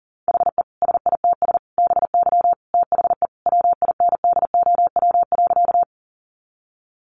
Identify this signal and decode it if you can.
Morse code, heard as a keyed tone, about 40 wpm, 700 Hz. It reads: HE SITS BY THE WINDOW.
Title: Easy Phrases 40wpm